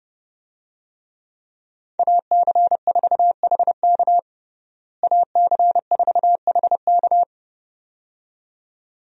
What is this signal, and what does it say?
Morse code, heard as a keyed tone, about 30 wpm, 700 Hz. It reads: AC4HK AC4HK
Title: USA Call Signs - 5 Characters - Repeated 2x - 30wpm